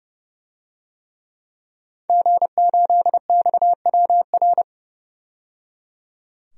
Morse code, heard as a keyed tone, about 30 wpm, 700 Hz. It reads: G8XWR
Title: DXCC-100 Call Signs - 5 Characters - 30wpm